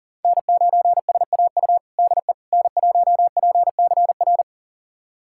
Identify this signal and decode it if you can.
Morse code, heard as a keyed tone, about 37 wpm, 700 Hz. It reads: N9SAU DE N1PCR
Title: USA Callsign DE Callsign - 5 Characters - 40wpm